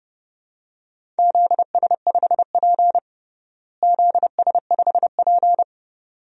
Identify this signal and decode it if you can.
Morse code, heard as a keyed tone, about 32 wpm, 700 Hz. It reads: ZS5P ZS5P
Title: DXCC-100 Call Signs - 4 Characters - Repeated 2x - 30wpm